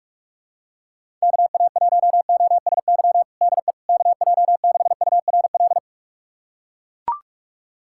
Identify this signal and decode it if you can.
Morse code, heard as a keyed tone, about 41 wpm, 700 Hz. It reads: KA1OSY DE KJ6URL
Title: USA Callsign DE Callsign - 6 Characters - 45wpm